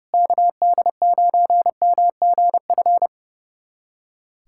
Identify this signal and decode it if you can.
Morse code, heard as a keyed tone, about 30 wpm, 700 Hz. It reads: KD9MGF
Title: USA Callsign DE Callsign - 6 Characters - 30wpm